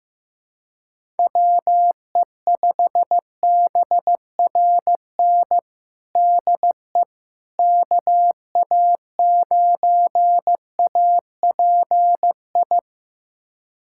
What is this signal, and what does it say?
Morse code, heard as a keyed tone, about 15 wpm, 700 Hz. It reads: WE5BRN DE KA9API
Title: USA Callsign DE Callsign - 6 Characters - 15wpm